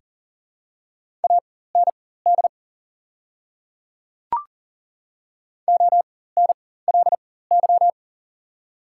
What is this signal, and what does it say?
Morse code, heard as a keyed tone, about 40 wpm, spaced as for 19 wpm, 690 Hz. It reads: AND ONLY
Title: 4x Character Spacing - Top 300 Words - 40wpm